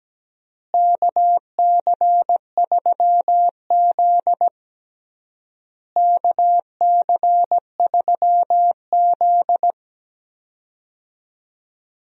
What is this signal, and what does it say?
Morse code, heard as a keyed tone, about 17 wpm, 700 Hz. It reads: KC3Z KC3Z
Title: USA Call Signs - 4 Characters - Repeated 3x - 17wpm